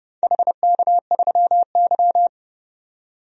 Morse code, heard as a keyed tone, about 31 wpm, 700 Hz. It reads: HK3Y
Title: DXCC-100 Call Signs - 4 Characters - 30wpm